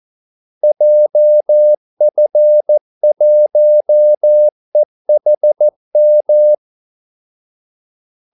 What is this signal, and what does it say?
Morse code, heard as a keyed tone, about 14 wpm, 600 Hz. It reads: JF1EHM
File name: Callsigns-14wpm